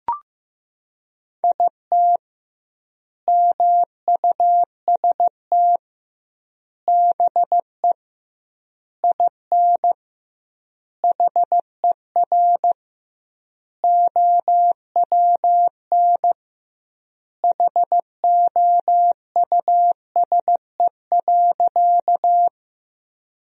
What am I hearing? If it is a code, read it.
Morse code, heard as a keyed tone, about 16 wpm, 700 Hz. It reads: IT MUST BE IN HER OWN HOUSE.
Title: 2x Word-Spacing - Sentences from Top 300 Words - 15wpm